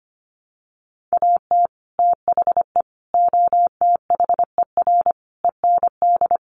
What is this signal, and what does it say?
Morse code, heard as a keyed tone, about 25 wpm, 700 Hz. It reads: AT THE OTHER END
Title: Sets of 5 Words 25wpm